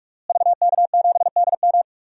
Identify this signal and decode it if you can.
Morse code, heard as a keyed tone, about 46 wpm, 700 Hz. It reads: VK7DM
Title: DXCC-100 Call Signs - 5 Characters - Repeated 2x - 45wpm